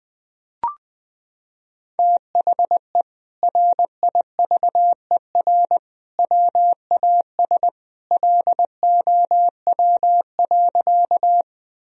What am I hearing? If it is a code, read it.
Morse code, heard as a keyed tone, about 21 wpm, 700 Hz. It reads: THE RIVER WAS LOW.
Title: Sentences from Top 1000 Words - Review Part 1 - 20wpm